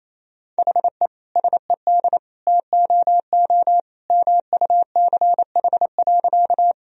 Morse code, heard as a keyed tone, about 28 wpm, 700 Hz. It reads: HE SED TOO MUCH.
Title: Sentences from Top 200 Words and Abbreviations 28wpm